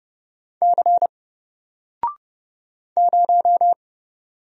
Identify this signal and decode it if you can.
Morse code, heard as a keyed tone, about 30 wpm, 700 Hz. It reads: C 0
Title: Single Letter-Number Alt 30wpm